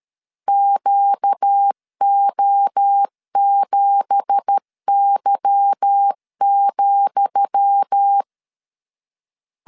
Morse code, heard as a keyed tone, about 13 wpm, 800 Hz. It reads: QO7Y,
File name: LEZIONI / 10 SIGNS / 10D COMA 12wpm 2544